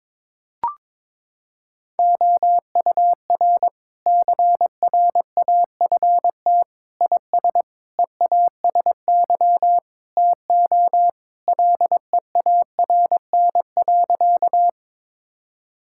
Morse code, heard as a keyed tone, about 22 wpm, 700 Hz. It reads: OUR CRAFT IS EASY TO LEARN.
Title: Sentences from Top 1000 Words - Review Part 6 - 22wpm